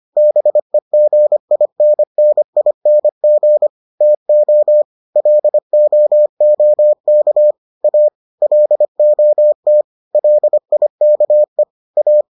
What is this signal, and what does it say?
Morse code, heard as a keyed tone, about 25 wpm, 600 Hz. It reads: BEGINNING TO LOOK A LOT LIKE A